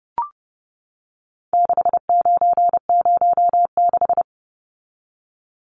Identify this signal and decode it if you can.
Morse code, heard as a keyed tone, about 30 wpm, 700 Hz. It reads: 6906
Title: Numbers - 4-Digit 30wpm